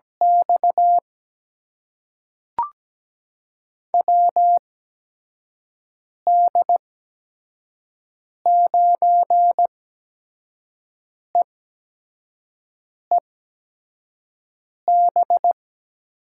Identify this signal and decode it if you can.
Morse code, heard as a keyed tone, about 17 wpm, spaced as for 5 wpm, 700 Hz. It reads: X WD9EEB
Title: USA Call Signs - 8x Character Spacing - 17wpm